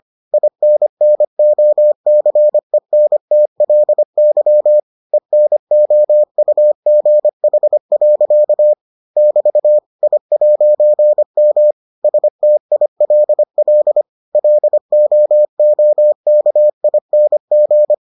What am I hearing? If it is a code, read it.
Morse code, heard as a keyed tone, about 25 wpm, 600 Hz. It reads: INNOCENTLY ENOUGH. <BT> I'M STILL LOOKING